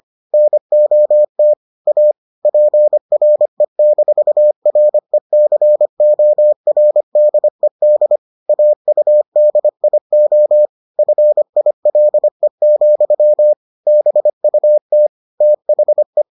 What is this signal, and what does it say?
Morse code, heard as a keyed tone, about 25 wpm, 600 Hz. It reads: NOT A PRE-RECORDED AUDIO FILE, BUT THE